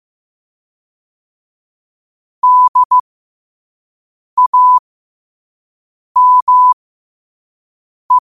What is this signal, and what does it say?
Morse code, heard as a keyed tone, about 15 wpm, spaced as for 5 wpm, 1000 Hz. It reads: DAME